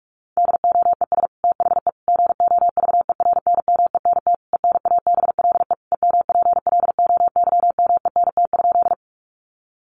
Morse code, heard as a keyed tone, about 44 wpm, 700 Hz. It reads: DOES THE GOVERNMENT ENABLE EMPLOYMENT?
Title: Sentences from Top 1500 Words 45wpm